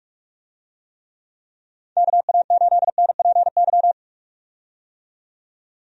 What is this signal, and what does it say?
Morse code, heard as a keyed tone, about 46 wpm, 700 Hz. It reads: KA8NPY
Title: USA Call Signs - 6 Characters - 45wpm